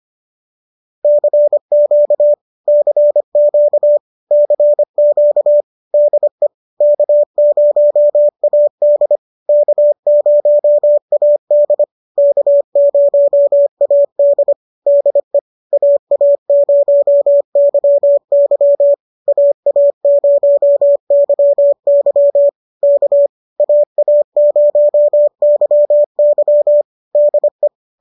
Morse code, heard as a keyed tone, about 25 wpm, 590 Hz. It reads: CQ CQ CQ DE K0AD K0AD K0AD DE AA0YY AA0YY K AA0YY DE